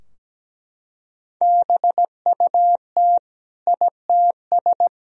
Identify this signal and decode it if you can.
Morse code, heard as a keyed tone, about 17 wpm, 700 Hz. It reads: BUT ITS
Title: Sentences from Top 1000 Words - Review Part 2 - 17wpm